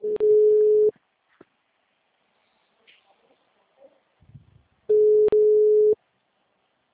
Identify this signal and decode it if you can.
Ringback tone (Europe, 425 Hz).
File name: out- -2037- - - .1473